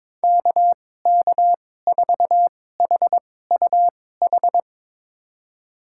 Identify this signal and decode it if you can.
Morse code, heard as a keyed tone, about 22 wpm, spaced as for 16 wpm, 700 Hz. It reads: KK4HUH
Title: USA Call Signs - 6 Characters - 2x Character Spacing - 22wpm